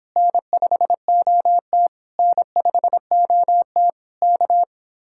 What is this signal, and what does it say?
Morse code, heard as a keyed tone, about 26 wpm, 700 Hz. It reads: N5OT N5OT K